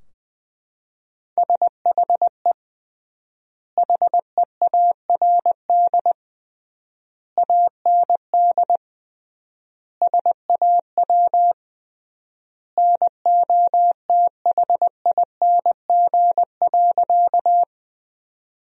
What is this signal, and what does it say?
Morse code, heard as a keyed tone, about 20 wpm, 700 Hz. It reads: SHE HEARD AND SAW NOTHING.
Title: 3x Word-Spacing - Sentences from Top 500 Words - 20wpm